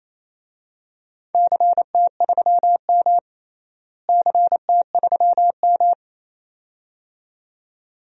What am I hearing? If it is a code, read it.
Morse code, heard as a keyed tone, about 28 wpm, 700 Hz. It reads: CT3M CT3M
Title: DXCC-100 Call Signs - 4 Characters - Repeated 2x - 28wpm